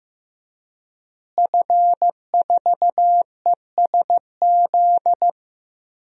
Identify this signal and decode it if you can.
Morse code, heard as a keyed tone, about 15 wpm, 700 Hz. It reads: F4ESZ
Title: DXCC-100 Call Signs - 5 Characters - Repeated 2x - 15wpm